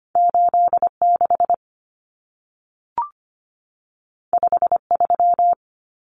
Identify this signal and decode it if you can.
Morse code, heard as a keyed tone, about 25 wpm, 700 Hz. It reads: 86 53
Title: Common Cut Numbers - 2-Digit 25wpm